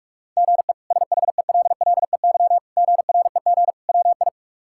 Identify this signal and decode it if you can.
Morse code, heard as a keyed tone, about 46 wpm, 700 Hz. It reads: GE SHELLEY GREG WI
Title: K1USN SST QSO Practice - 45wpm